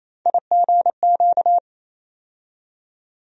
Morse code, heard as a keyed tone, about 28 wpm, 700 Hz. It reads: IGQ
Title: USA Call Sign Postfixes - 3 Letters 28wpm